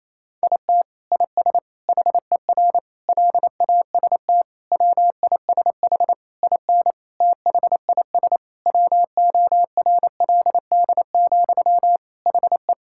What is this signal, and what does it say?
Morse code, heard as a keyed tone, about 27 wpm, 700 Hz. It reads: IT IS HER LAST WISH IN THIS WORLD, HE
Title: Sentences from Top 1600 Words 28wpm